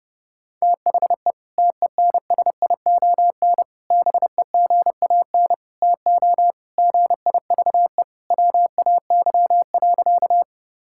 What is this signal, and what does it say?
Morse code, heard as a keyed tone, about 32 wpm, 700 Hz. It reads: THE TENSION BEGAN TO GIVE WAY.
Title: Sentences from Top 1600 Words 30wpm